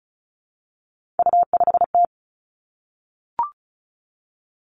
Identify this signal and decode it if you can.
Morse code, heard as a keyed tone, about 36 wpm, 700 Hz. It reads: U5T
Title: Three Challenging Letter-Number Combinations 35wpm